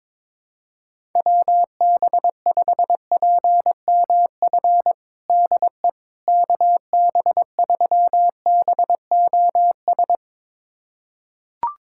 Morse code, heard as a keyed tone, about 22 wpm, 700 Hz. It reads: WB5PMF DE KB3BOS
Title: USA Callsign DE Callsign 22wpm